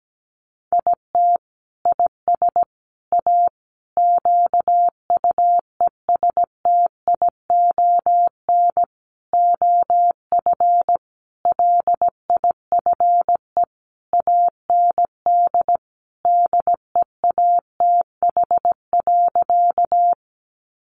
Morse code, heard as a keyed tone, about 17 wpm, 700 Hz. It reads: IT IS A QUESTION OF LIFE AND DEATH.
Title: Sentences from Top 1100 Words 17wpm